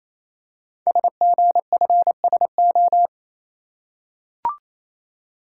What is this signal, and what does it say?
Morse code, heard as a keyed tone, about 29 wpm, 700 Hz. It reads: SGFSO
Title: Code Groups - 5 Letters - 28wpm